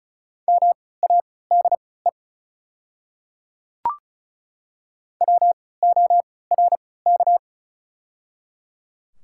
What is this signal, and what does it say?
Morse code, heard as a keyed tone, about 36 wpm, spaced as for 20 wpm, 700 Hz. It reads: MADE WORK
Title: 3x Character Spacing - Top 200 Words - 35wpm